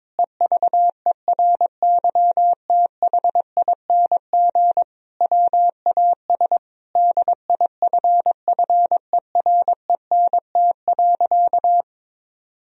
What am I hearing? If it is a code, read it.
Morse code, heard as a keyed tone, about 22 wpm, 700 Hz. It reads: EVERYTHING WAS DIFFERENT.
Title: Sentences from Top 1000 Words - Review Part 3 - 22wpm